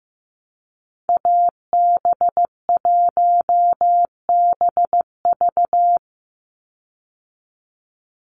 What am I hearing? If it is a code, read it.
Morse code, heard as a keyed tone, about 15 wpm, 700 Hz. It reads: AB1BV
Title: USA Call Signs 15wpm